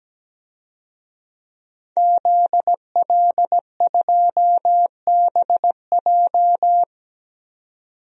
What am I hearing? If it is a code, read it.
Morse code, heard as a keyed tone, about 17 wpm, 700 Hz. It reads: ZL2BJ
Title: DXCC-100 Call Signs - 5 Characters - 17wpm